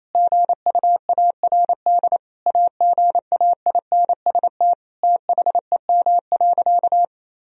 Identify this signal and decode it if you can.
Morse code, heard as a keyed tone, about 28 wpm, 700 Hz. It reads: GUARD AGAINST THEM.
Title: Sentences from Top 1000 Words - Review Part 4 - 28wpm